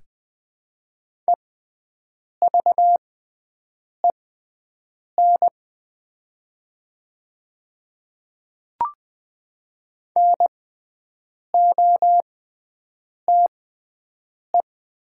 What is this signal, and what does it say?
Morse code, heard as a keyed tone, about 20 wpm, spaced as for 7 wpm, 700 Hz. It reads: EVEN NOTE
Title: 6x Character Spacing - Top 500 Words - 20wpm